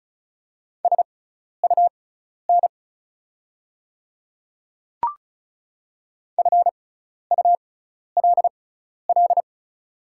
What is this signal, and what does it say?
Morse code, heard as a keyed tone, about 36 wpm, spaced as for 12 wpm, 700 Hz. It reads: SUN FULL
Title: 6x Character Spacing - Top 500 Words - 35wpm